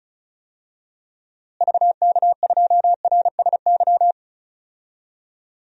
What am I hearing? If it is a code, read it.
Morse code, heard as a keyed tone, about 36 wpm, 700 Hz. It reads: VK2RSY
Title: DXCC-100 Call Signs - 6 Characters - Repeated 2x - 35wpm